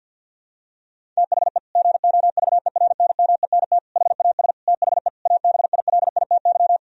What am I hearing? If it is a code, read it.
Morse code, heard as a keyed tone, about 50 wpm, 700 Hz. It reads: THE GOVERNMENT HAS THE ABILITY
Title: Sentences from Top 1000 Words - Review Part 3 - 50wpm